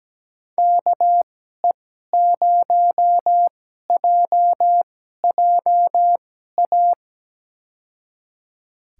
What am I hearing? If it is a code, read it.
Morse code, heard as a keyed tone, about 17 wpm, spaced as for 12 wpm, 700 Hz. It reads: KE0JJA
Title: USA Call Signs - 2x Character Spacing - 17wpm